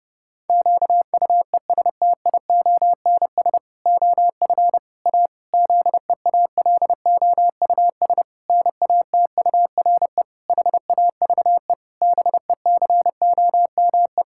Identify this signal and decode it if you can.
Morse code, heard as a keyed tone, about 30 wpm, 700 Hz. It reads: QUESTIONS OF A ZEALOUS NATURE HAVE BECOME
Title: Pangram Challenge v2 - 30wpm